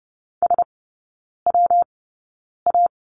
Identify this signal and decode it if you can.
Morse code, heard as a keyed tone, about 30 wpm, spaced as for 9 wpm, 700 Hz. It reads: SWA
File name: Lesson 19e - Letter w - Three Characters 30wpm